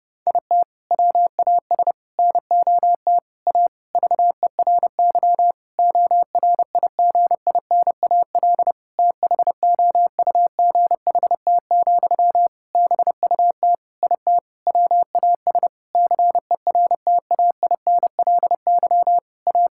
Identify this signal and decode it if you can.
Morse code, heard as a keyed tone, about 30 wpm, 700 Hz. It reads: IT WAS NOT A VERY ORIGINAL THOUGHT, BUT IT WAS CERTAINLY A
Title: Sentences from Top 1000 Words - Review Part 6 - 30wpm